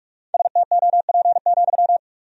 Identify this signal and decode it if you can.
Morse code, heard as a keyed tone, about 46 wpm, 700 Hz. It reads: STOP,
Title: 2x Word-Spacing - Sentences from Top 300 Words - 45wpm